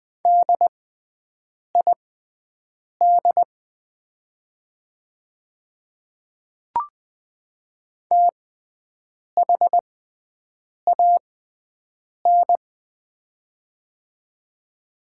Morse code, heard as a keyed tone, about 20 wpm, spaced as for 7 wpm, 700 Hz. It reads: DID THAN
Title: 6x Character Spacing - Top 200 Words - 20wpm